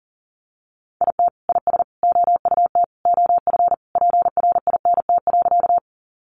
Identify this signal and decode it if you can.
Morse code, heard as a keyed tone, about 40 wpm, 700 Hz. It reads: IT IS OUT OF PRINT.
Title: Sentences from Top 1400 Words 40wpm